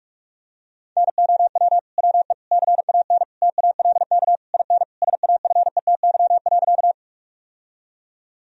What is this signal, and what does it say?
Morse code, heard as a keyed tone, about 42 wpm, 700 Hz. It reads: NOW WE CAN TALK IN SAFETY.
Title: Sentences from Top 1000 Words - Review Part 4 - 45wpm